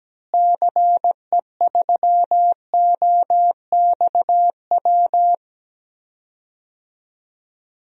Morse code, heard as a keyed tone, about 17 wpm, 700 Hz. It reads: CE3OXW
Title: DXCC-100 Call Signs - 6 Characters - Repeated 3x - 17wpm